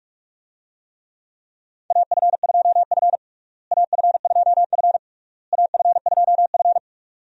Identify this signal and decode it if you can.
Morse code, heard as a keyed tone, about 42 wpm, 700 Hz. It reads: AF2F AF2F AF2F
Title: USA Call Signs - 4 Characters - Repeated 3x - 45wpm